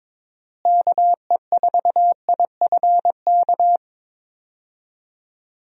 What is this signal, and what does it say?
Morse code, heard as a keyed tone, about 22 wpm, 700 Hz. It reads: KE4IFK
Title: USA Call Signs - 6 Characters - Repeated 3x - 22wpm